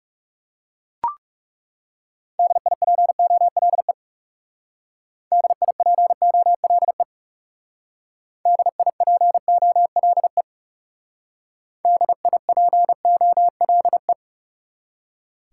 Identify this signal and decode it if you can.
Morse code, heard as a keyed tone, about 40 wpm, 700 Hz. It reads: DIPOLE DIPOLE DIPOLE DIPOLE